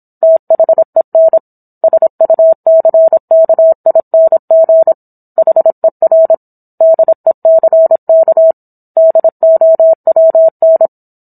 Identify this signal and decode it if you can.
Morse code, heard as a keyed tone, about 26 wpm, 650 Hz. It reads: THEN SUCKING HER DECK DOWN